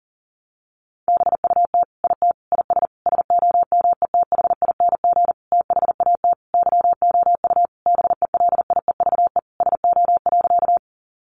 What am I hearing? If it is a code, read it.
Morse code, heard as a keyed tone, about 40 wpm, 700 Hz. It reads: BUT IT IS SOMETHING THAT YOU BELIEVE SO.
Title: Sentences from Top 1000 Words 40wpm